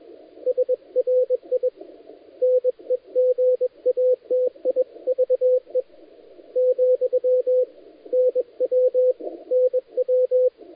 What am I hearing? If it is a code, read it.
Morse code, heard as a keyed tone, about 23 wpm, 500 Hz. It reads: SRI NEGATIVE , NW NW
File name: dlazaawansowanych / nagraniazpasma / titawaNETcontrol